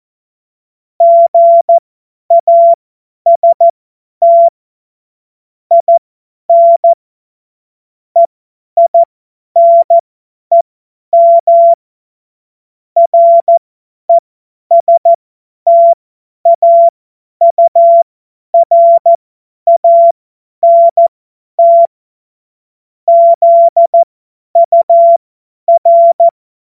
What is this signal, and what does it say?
Morse code, heard as a keyed tone, about 14 wpm, spaced as for 10 wpm, 680 Hz. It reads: GAST IN EINEM RESTAURANT ZUR
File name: Morsekurs / Witze / Witz 06. April 2021, 50 F70